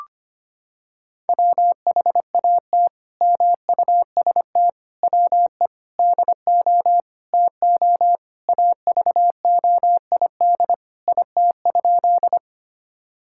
Morse code, heard as a keyed tone, about 25 wpm, 700 Hz. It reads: WHAT MUST WE DO TO AVOID IT?
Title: Sentences from Top 1000 Words - Review Part 6 - 25wpm